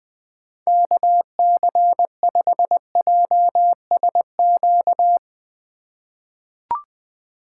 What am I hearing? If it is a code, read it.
Morse code, heard as a keyed tone, about 20 wpm, 700 Hz. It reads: KC5JSQ
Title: USA Call Signs - 6 Characters - 4x Character Spacing - 20wpm